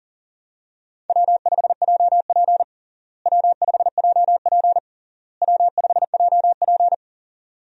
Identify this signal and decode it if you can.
Morse code, heard as a keyed tone, about 40 wpm, 700 Hz. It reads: W5JP W5JP W5JP
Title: USA Call Signs - 4 Characters - Repeated 3x - 40wpm